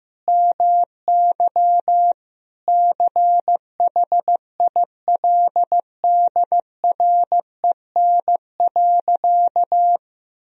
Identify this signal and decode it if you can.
Morse code, heard as a keyed tone, about 14 wpm, 700 Hz. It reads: MY CHILDREN.
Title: Sentences from Top 1000 Words - Review Part 3 - 15wpm